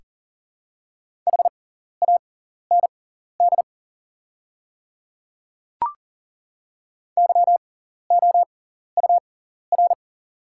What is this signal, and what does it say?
Morse code, heard as a keyed tone, about 37 wpm, spaced as for 14 wpm, 700 Hz. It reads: HAND YOUR
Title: 6x Character Spacing - Top 300 Words - 40wpm